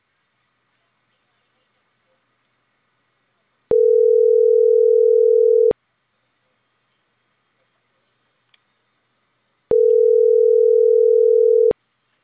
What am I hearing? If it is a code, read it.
Ringback tone (North America), 440 + 480 Hz.